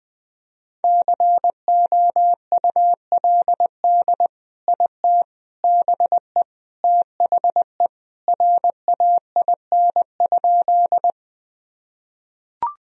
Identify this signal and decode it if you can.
Morse code, heard as a keyed tone, about 21 wpm, 700 Hz. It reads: COULD IT BE THE RAIN?
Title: Sentences from Top 1000 Words - Review Part 1 - 20wpm